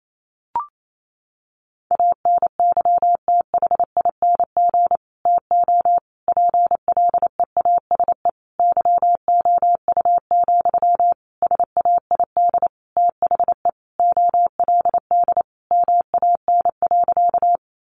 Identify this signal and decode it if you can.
Morse code, heard as a keyed tone, about 28 wpm, 700 Hz. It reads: ANYTHING TO PLEASE YOU, SAID THE OLD MAN.
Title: Sentences from Top 1500 Words 28wpm